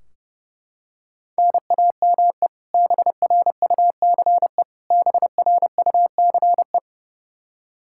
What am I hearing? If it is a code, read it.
Morse code, heard as a keyed tone, about 30 wpm, 700 Hz. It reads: NAME BRUCE BRUCE
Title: Practice QSO - Exchange 1 - Beginning - 30wpm